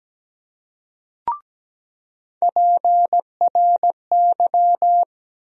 Morse code, heard as a keyed tone, about 17 wpm, 700 Hz. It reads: PRY
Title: USA Call Sign Postfixes - 3 Letters 17wpm